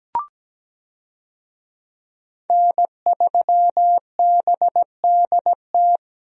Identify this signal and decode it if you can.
Morse code, heard as a keyed tone, about 17 wpm, 700 Hz. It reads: N3BDT
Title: USA Call Signs - 5 Characters - 17wpm